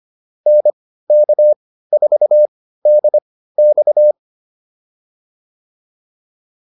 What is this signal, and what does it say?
Morse code, heard as a keyed tone, about 25 wpm, spaced as for 15 wpm, 600 Hz. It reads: NK4DX
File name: QSO110 15